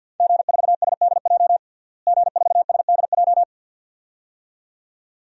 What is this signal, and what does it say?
Morse code, heard as a keyed tone, about 52 wpm, 700 Hz. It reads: G4SDJ G4SDJ
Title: DXCC-100 Call Signs - 5 Characters - Repeated 2x - 50wpm